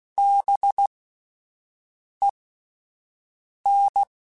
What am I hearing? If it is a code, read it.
Morse code, heard as a keyed tone, about 16 wpm, spaced as for 5 wpm, 800 Hz. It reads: BEN
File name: Alphabetical / Alphabetical mp3s / Bs @5